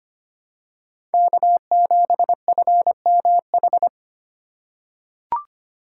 Morse code, heard as a keyed tone, about 25 wpm, 700 Hz. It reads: K7FMH
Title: USA Call Signs - 5 Characters - 3x Character Spacing - 25wpm